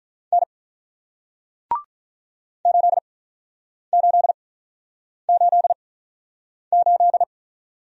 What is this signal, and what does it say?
Morse code, heard as a keyed tone, about 48 wpm, spaced as for 9 wpm, 700 Hz. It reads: N 8888